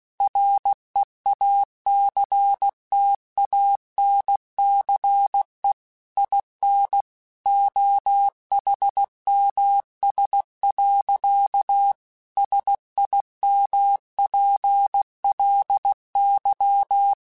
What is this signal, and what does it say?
Morse code, heard as a keyed tone, about 16 wpm, 800 Hz. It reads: REACTANCE IN OHMS. SIMPLY
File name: Technical Subjects / Technical Subjects-mp3s / XC @15